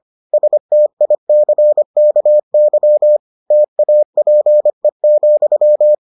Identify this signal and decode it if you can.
Morse code, heard as a keyed tone, about 25 wpm, 600 Hz. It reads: STICKY TAPE,